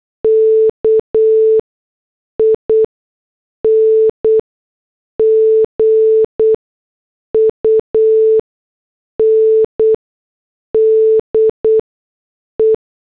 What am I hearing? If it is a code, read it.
Morse code, heard as a keyed tone, about 8 wpm, spaced as for 6 wpm, 440 Hz. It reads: KING UNDE
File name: morse / headlines